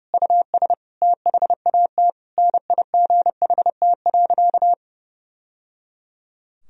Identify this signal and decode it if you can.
Morse code, heard as a keyed tone, about 30 wpm, 700 Hz. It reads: US THAT NIGHT.
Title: Sentences from Top 1000 Words - Review Part 2 - 30wpm